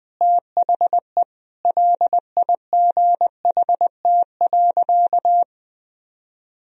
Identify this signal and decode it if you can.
Morse code, heard as a keyed tone, about 21 wpm, 700 Hz. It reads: THE LIGHT.
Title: Sentences from Top 1000 Words - Review Part 2 - 20wpm